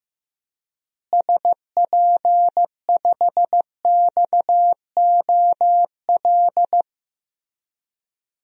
Morse code, heard as a keyed tone, about 15 wpm, 700 Hz. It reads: SP5XOL
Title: DXCC-100 Call Signs - 6 Characters - Repeated 3x - 15wpm